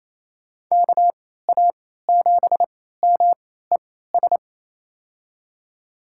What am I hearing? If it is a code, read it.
Morse code, heard as a keyed tone, about 28 wpm, spaced as for 16 wpm, 700 Hz. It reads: KA7MES
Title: USA Call Signs - 6 Characters - 3x Character Spacing - 28wpm